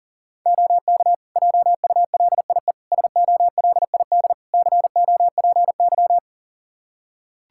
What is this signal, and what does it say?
Morse code, heard as a keyed tone, about 40 wpm, 700 Hz. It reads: OK JULIE SOLID COPY
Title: Practice QSO - Exchange 3 - More Info - 40wpm